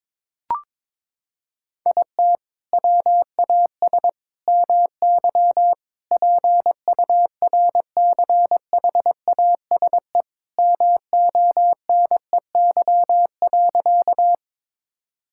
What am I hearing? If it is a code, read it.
Morse code, heard as a keyed tone, about 21 wpm, 700 Hz. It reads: IT WAS MY PURCHASE MONEY.
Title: Sentences from Top 1000 Words - Review Part 5 - 22wpm